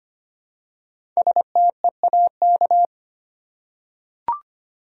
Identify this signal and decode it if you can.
Morse code, heard as a keyed tone, about 25 wpm, 690 Hz. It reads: STEAK
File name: practice / 25wpm / Easy Words with Hints - 25wpm